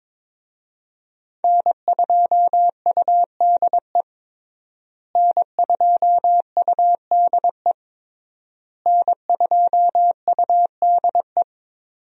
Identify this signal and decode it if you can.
Morse code, heard as a keyed tone, about 22 wpm, 700 Hz. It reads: N2UDE N2UDE N2UDE
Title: USA Call Signs - 5 Characters - Repeated 3x - 22wpm